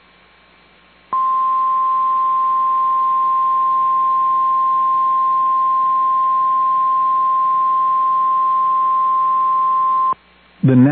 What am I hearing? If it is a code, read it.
Emergency Alert System (SAME) tones (the 1050 Hz attention tone). It is heard from 1.2 to 9.9 s.